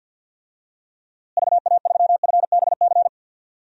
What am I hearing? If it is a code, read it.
Morse code, heard as a keyed tone, about 47 wpm, 700 Hz. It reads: VA3FBC
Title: DXCC-100 Call Signs - 6 Characters - Repeated 2x - 50wpm